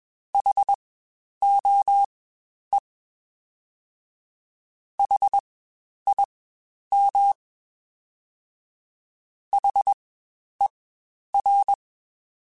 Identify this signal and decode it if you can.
Morse code, heard as a keyed tone, about 21 wpm, spaced as for 10 wpm, 800 Hz. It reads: HOE HIM HER
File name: Spaced Alphas / Spaced Alphas mp3s / Hs sp @20